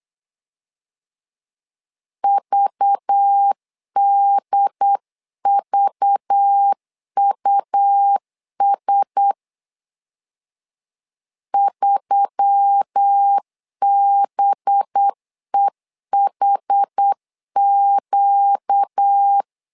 Morse code, heard as a keyed tone, about 8 wpm, 800 Hz. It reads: VDVUS 3BEHQ
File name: LEZIONI / 5 LBVC / 5 LBV 8wpm 2837